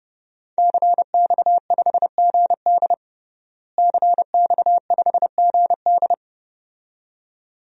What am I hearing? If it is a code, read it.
Morse code, heard as a keyed tone, about 32 wpm, 700 Hz. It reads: CX5GD CX5GD
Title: DXCC-100 Call Signs - 5 Characters - Repeated 2x - 30wpm